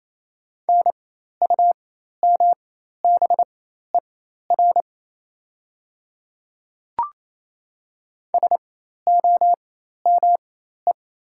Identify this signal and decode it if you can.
Morse code, heard as a keyed tone, about 28 wpm, spaced as for 13 wpm, 690 Hz. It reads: NUMBER SOME
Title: 4x Character Spacing - Top 100 Words - 28wpm